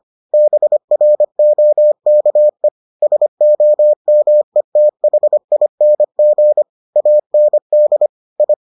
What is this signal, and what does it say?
Morse code, heard as a keyed tone, about 25 wpm, 600 Hz. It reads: BROKE SOMETHING AND I